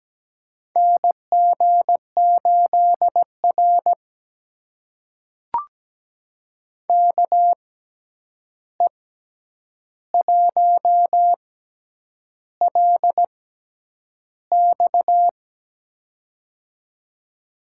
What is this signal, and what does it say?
Morse code, heard as a keyed tone, about 17 wpm, spaced as for 8 wpm, 700 Hz. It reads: NG8R KE1LX
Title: USA Call Signs - 6x Character Spacing - 17wpm